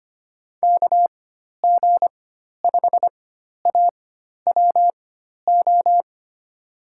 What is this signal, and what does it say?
Morse code, heard as a keyed tone, about 25 wpm, spaced as for 12 wpm, 700 Hz. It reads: KG5AWO
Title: USA Call Signs - 6 Characters - 4x Character Spacing - 25wpm